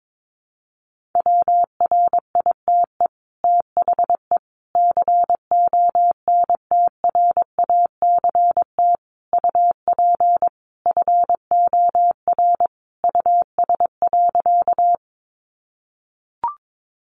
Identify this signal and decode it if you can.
Morse code, heard as a keyed tone, about 22 wpm, 700 Hz. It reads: WRITE THE CONTRACT UP FOR US.
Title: Sentences from Top 1400 Words 22wpm